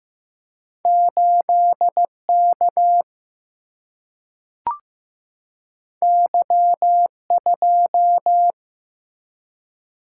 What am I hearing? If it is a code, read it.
Morse code, heard as a keyed tone, about 15 wpm, 700 Hz. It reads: 8 K Y 2
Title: Single-Letter Single-Number 15wpm